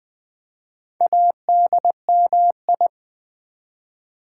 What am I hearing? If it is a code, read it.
Morse code, heard as a keyed tone, about 20 wpm, 700 Hz. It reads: ADMI
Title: Build A Word - 4 Letters At A Time - 10-Letter Words - 20wpm